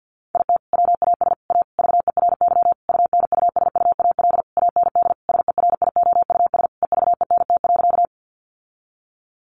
Morse code, heard as a keyed tone, about 50 wpm, 700 Hz. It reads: IT WAS A VERY UNUSUAL AND SERIOUS EVENT.
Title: Sentences from Top 1400 Words 50wpm